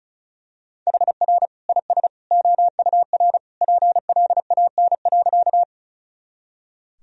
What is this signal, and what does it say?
Morse code, heard as a keyed tone, about 36 wpm, 700 Hz. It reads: HR IS OUR PLAN.
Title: Sentences from Top 500 Words and Abbreviations 35wpm